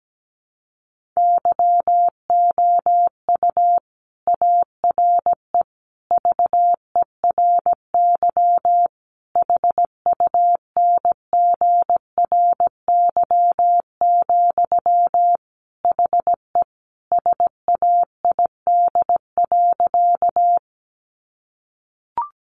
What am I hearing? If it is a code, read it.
Morse code, heard as a keyed tone, about 17 wpm, 700 Hz. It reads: YOU ARE VERY HUNGRY, HE SAID.
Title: Sentences from Top 1400 Words 17wpm